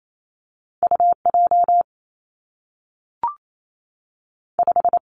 Morse code, heard as a keyed tone, about 28 wpm, 700 Hz. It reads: U J 5
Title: Two Challenging Letter-Number Combinations 28wpm